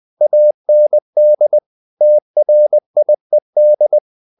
Morse code, heard as a keyed tone, about 20 wpm, 600 Hz. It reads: AND TRIED
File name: the old man and the sea part 2 20wpm